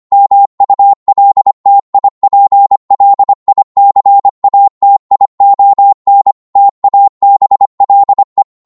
Morse code, heard as a keyed tone, about 25 wpm, 800 Hz. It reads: MULTIPLICATION TABLE